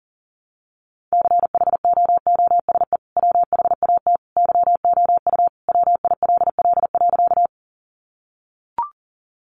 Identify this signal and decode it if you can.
Morse code, heard as a keyed tone, about 40 wpm, 700 Hz. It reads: CHOOSE WHAT YOU WILL.
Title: Sentences from Top 700 Words 40wpm